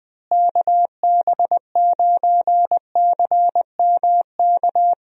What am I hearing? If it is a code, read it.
Morse code, heard as a keyed tone, about 20 wpm, 700 Hz. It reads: KB9CMK
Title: USA Call Signs - 6 Characters - Repeated 2x - 20wpm